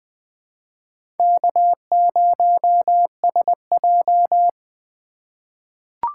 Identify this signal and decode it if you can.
Morse code, heard as a keyed tone, about 20 wpm, 700 Hz. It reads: K0SJ
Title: USA Call Signs - 8x Character Spacing - 20wpm